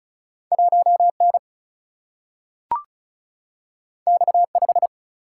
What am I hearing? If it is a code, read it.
Morse code, heard as a keyed tone, about 36 wpm, 700 Hz. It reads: 1 N X 5
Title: Single-Letter Single-Number 35wpm